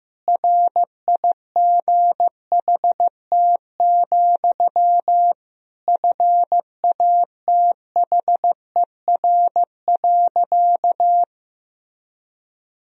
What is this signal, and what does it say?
Morse code, heard as a keyed tone, about 15 wpm, 700 Hz. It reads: RIGHT, FATHER.
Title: Sentences from Top 300 Words and Abbreviations 15wpm